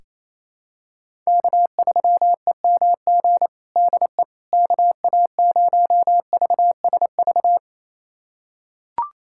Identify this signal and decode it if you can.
Morse code, heard as a keyed tone, about 28 wpm, 700 Hz. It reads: K3EMG DE KA0VSV
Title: USA Callsign DE Callsign 28wpm